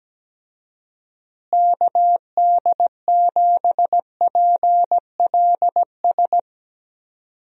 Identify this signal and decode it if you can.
Morse code, heard as a keyed tone, about 17 wpm, 700 Hz. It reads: KD7PLS
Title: USA Call Signs - 6 Characters - Repeated 2x - 17wpm